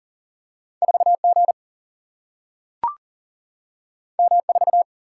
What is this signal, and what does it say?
Morse code, heard as a keyed tone, about 40 wpm, 700 Hz. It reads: 4 G M 4
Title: Single-Letter Single-Number 40wpm